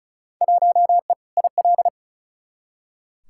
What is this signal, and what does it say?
Morse code, heard as a keyed tone, about 36 wpm, 700 Hz. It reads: 1E IL
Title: Field Day Exchanges - 35wpm